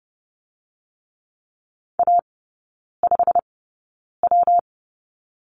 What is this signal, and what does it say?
Morse code, heard as a keyed tone, about 30 wpm, 700 Hz. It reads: A 5 W
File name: Lesson 19e - Letter w - Three Characters 30wpm